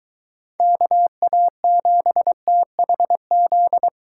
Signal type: Morse code

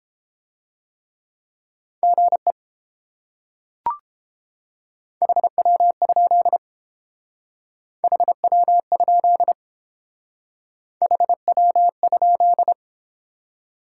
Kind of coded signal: Morse code

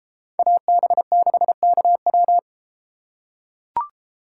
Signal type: Morse code

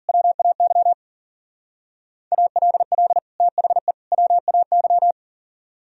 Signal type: Morse code

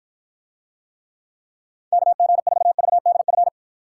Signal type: Morse code

